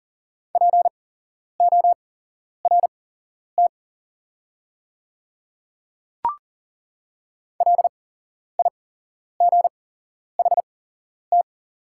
Morse code, heard as a keyed tone, about 40 wpm, spaced as for 11 wpm, 690 Hz. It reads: PORT LIGHT
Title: 8x Character Spacing - Top 200 Words - 40wpm